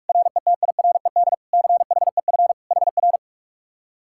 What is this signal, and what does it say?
Morse code, heard as a keyed tone, about 59 wpm, 700 Hz. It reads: RETIRED CHEF HR